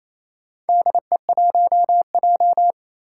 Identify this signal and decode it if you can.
Morse code, heard as a keyed tone, about 28 wpm, 700 Hz. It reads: DE1J
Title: Four Challenging Letter-Number Combinations 28wpm